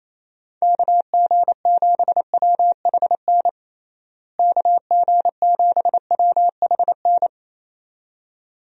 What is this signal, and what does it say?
Morse code, heard as a keyed tone, about 28 wpm, 700 Hz. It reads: KG7WHN KG7WHN
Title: USA Call Signs - 6 Characters - Repeated 2x - 28wpm